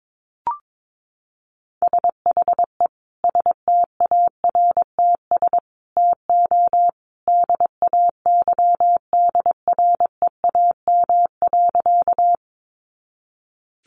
Morse code, heard as a keyed tone, about 22 wpm, 700 Hz. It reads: SHE STARTS TO DAYDREAM.
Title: Easy Phrases 22wpm